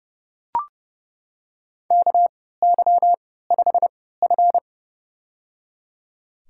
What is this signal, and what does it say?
Morse code, heard as a keyed tone, about 30 wpm, spaced as for 17 wpm, 700 Hz. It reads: KY5F
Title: USA Call Signs - 4 Characters - 3x Character Spacing - 30wpm